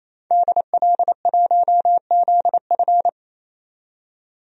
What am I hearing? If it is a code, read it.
Morse code, heard as a keyed tone, about 28 wpm, 700 Hz. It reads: DL1ZF
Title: DXCC-100 Call Signs - 5 Characters - 28wpm